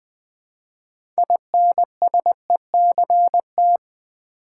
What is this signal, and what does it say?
Morse code, heard as a keyed tone, about 20 wpm, 700 Hz. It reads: INSECT
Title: Build A Word - 3 Letters At A Time - 6-Letter Words - 20wpm